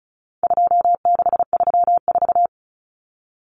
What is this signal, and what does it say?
Morse code, heard as a keyed tone, about 34 wpm, 700 Hz. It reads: 2634
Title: Numbers - 4-Digit 35wpm